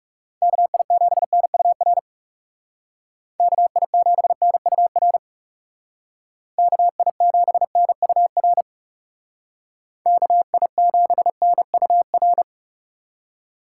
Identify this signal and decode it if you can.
Morse code, heard as a keyed tone, about 36 wpm, 700 Hz. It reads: KI7NUR KI7NUR KI7NUR KI7NUR
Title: Speed Racing - USA 2x3 Call Signs 30wpm